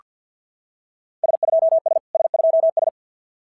Morse code, heard as a keyed tone, about 46 wpm, 660 Hz. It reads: S2S S2S
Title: SOTA QSO Practice 50wpm